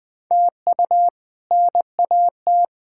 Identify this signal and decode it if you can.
Morse code, heard as a keyed tone, about 19 wpm, 700 Hz. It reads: TU NAT
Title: K1USN SST QSO Practice - 20wpm